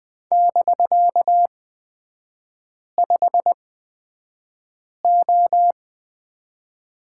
Morse code, heard as a keyed tone, about 20 wpm, spaced as for 5 wpm, 700 Hz. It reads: <BK>5O
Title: 250 Lesson 40e - BK - Three Characters 20wpm